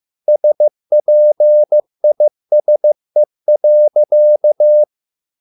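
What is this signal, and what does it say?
Morse code, heard as a keyed tone, about 15 wpm, 600 Hz. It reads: SPISE.
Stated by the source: Morse code at a slow 15 wpm from CW med Gnister.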